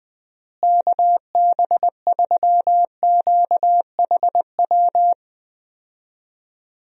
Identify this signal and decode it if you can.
Morse code, heard as a keyed tone, about 20 wpm, 700 Hz. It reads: KB3QHW
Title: USA Call Signs - 6 Characters - Repeated 2x - 20wpm